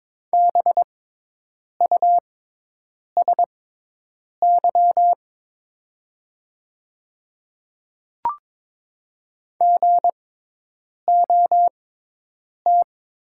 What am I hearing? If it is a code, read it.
Morse code, heard as a keyed tone, about 22 wpm, spaced as for 8 wpm, 700 Hz. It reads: BUSY GOT
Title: 6x Character Spacing - Top 500 Words - 22wpm